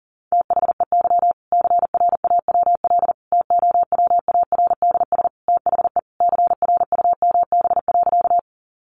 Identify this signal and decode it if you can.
Morse code, heard as a keyed tone, about 40 wpm, 700 Hz. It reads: THEY CRAWL TOWARDS THE CRUMB.
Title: Additional Easy Phrases 40wpm